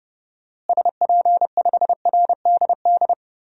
Morse code, heard as a keyed tone, about 30 wpm, 700 Hz. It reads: SP5RDD
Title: DXCC-100 Call Signs - 6 Characters - Repeated 2x - 30wpm